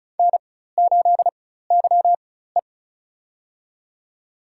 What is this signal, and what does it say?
Morse code, heard as a keyed tone, about 36 wpm, spaced as for 16 wpm, 700 Hz. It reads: N8YE
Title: USA Call Signs - 4 Characters - 4x Character Spacing - 35wpm